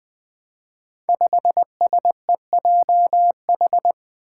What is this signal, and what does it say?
Morse code, heard as a keyed tone, about 20 wpm, 700 Hz. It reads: 5SEJH
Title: Five Challenging Letter-Number Combinations 20wpm